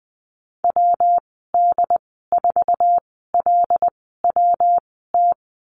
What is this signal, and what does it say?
Morse code, heard as a keyed tone, about 19 wpm, spaced as for 14 wpm, 700 Hz. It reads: WD4LWT
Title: 173 Lesson 28g - Number 9 - Call Signs 20wpm